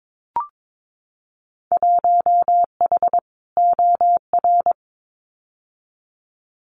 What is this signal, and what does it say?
Morse code, heard as a keyed tone, about 22 wpm, 700 Hz. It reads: 1H OR
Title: Winter Field Day Exchanges 22wpm